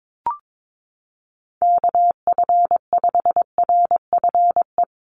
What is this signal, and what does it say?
Morse code, heard as a keyed tone, about 22 wpm, 700 Hz. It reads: KF5RFE
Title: USA Call Signs 22wpm